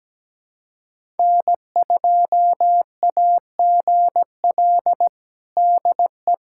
Morse code, heard as a keyed tone, about 17 wpm, 700 Hz. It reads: N2AGL DE
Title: USA Callsign DE Callsign - 5 Characters - 17wpm